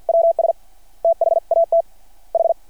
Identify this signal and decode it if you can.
Morse code, heard as a keyed tone, about 48 wpm, 660 Hz. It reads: WS THAT H
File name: archived / CRSnet